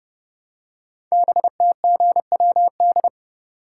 Morse code, heard as a keyed tone, about 28 wpm, 700 Hz. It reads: BTGWD
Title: Code Groups - 5 Letters - 30wpm